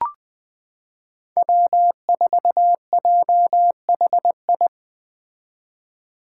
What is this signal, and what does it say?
Morse code, heard as a keyed tone, about 20 wpm, 700 Hz. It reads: W4JHI
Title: Five Challenging Letter-Number Combinations 20wpm